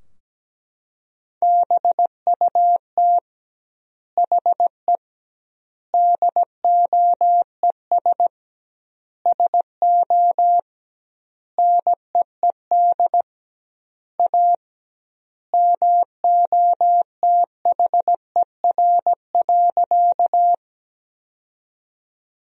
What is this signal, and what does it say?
Morse code, heard as a keyed tone, about 17 wpm, 700 Hz. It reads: BUT HE DOES SO NEED A MOTHER.
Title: 2x Word-Spacing - Sentences from Top 200 Words - 17wpm